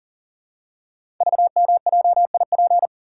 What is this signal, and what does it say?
Morse code, heard as a keyed tone, about 39 wpm, 700 Hz. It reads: VMJIP
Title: Code Groups - 5 Letter:Numbers - 40wpm